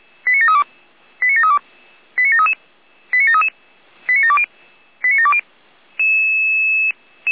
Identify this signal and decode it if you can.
Five-tone selective call (ZVEI), heard as quick runs of five tones, tones of about 70 ms. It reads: 89732 89732 89733 89733 89720 89720
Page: SDR/ZVEI